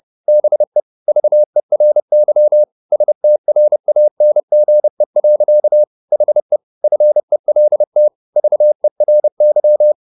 Morse code, heard as a keyed tone, about 30 wpm, 600 Hz. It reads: BE VERY STRANGE. HE FELT VERY
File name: the old man and the sea / the old man and the sea 1 30wpm